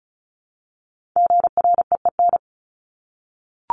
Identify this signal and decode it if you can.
Morse code, heard as a keyed tone, about 34 wpm, 700 Hz. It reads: GREEN
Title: Top 500 Words Repeated 35wpm